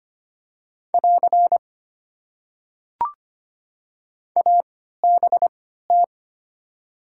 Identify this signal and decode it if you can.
Morse code, heard as a keyed tone, about 25 wpm, spaced as for 14 wpm, 700 Hz. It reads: <AR> ABT
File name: Top 90 QSO Elements - 3x Character Spacing - 25wpm